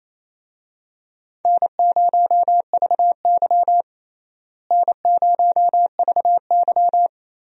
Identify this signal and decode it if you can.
Morse code, heard as a keyed tone, about 28 wpm, 700 Hz. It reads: N0VY N0VY
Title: USA Call Signs - 4 Characters - Repeated 2x - 28wpm